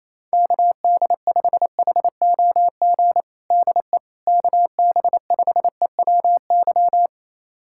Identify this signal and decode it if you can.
Morse code, heard as a keyed tone, about 28 wpm, 700 Hz. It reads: KD5HOG DE KB5EWY
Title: USA Callsign DE Callsign - 6 Characters - 28wpm